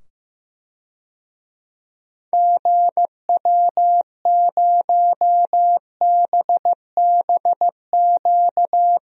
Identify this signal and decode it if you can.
Morse code, heard as a keyed tone, about 15 wpm, 700 Hz. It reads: GW0BBQ
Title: DXCC-100 Call Signs - 6 Characters - 15wpm